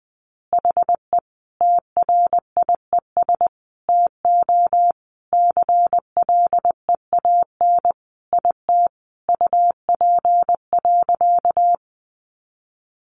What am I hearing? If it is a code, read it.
Morse code, heard as a keyed tone, about 20 wpm, 700 Hz. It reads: HE TRIES TO CLEAN IT UP.
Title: Additional Easy Phrases 20wpm